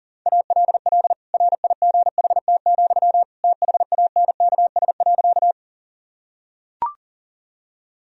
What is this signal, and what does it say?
Morse code, heard as a keyed tone, about 40 wpm, 700 Hz. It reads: ALL RIGHT, THANKS.
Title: Sentences from Top 1000 Words - Review Part 3 - 40wpm